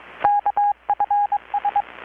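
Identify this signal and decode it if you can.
Morse code, heard as a keyed tone, about 21 wpm, 820 Hz, with radio noise. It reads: KFS
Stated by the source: Morse from KPH's Night of Nights XX maritime radiotelegraphy event, received in Chiapas, Mexico about 3,800 km away on a Kenwood R-600 with a grounded 10 m long-wire antenna.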